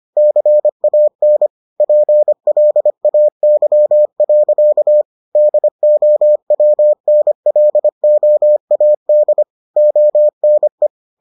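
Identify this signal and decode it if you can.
Morse code, heard as a keyed tone, about 25 wpm, 600 Hz. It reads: CAN PLAY. DOWNLOAD ONE
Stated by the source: Morse edition of the Foundations of Amateur Radio episodes.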